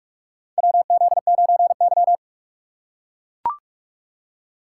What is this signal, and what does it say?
Morse code, heard as a keyed tone, about 44 wpm, 700 Hz. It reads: WZ9Y
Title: USA Call Signs - 4 Characters - 2x Character Spacing - 45wpm